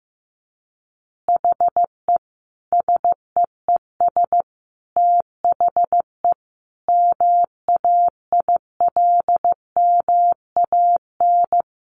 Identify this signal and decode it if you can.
Morse code, heard as a keyed tone, about 15 wpm, 700 Hz. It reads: HE SEES THE MAILMAN
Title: Additional Easy Phrases 15wpm